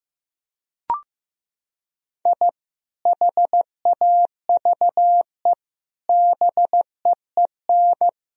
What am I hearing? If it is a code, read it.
Morse code, heard as a keyed tone, about 16 wpm, 700 Hz. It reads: I HAVE BEEN
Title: Sentences from Top 1000 Words - Review Part 1 - 15wpm